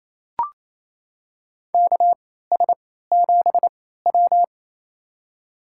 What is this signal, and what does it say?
Morse code, heard as a keyed tone, about 28 wpm, spaced as for 16 wpm, 700 Hz. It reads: KS7W
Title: USA Call Signs - 4 Characters - 3x Character Spacing - 28wpm